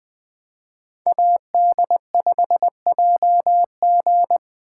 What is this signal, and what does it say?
Morse code, heard as a keyed tone, about 20 wpm, 700 Hz. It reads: AD5JG